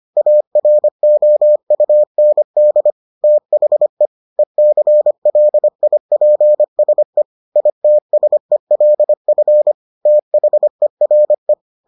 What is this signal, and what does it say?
Morse code, heard as a keyed tone, about 25 wpm, 600 Hz. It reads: AROUND THE ECLIPSE ITSELF THERE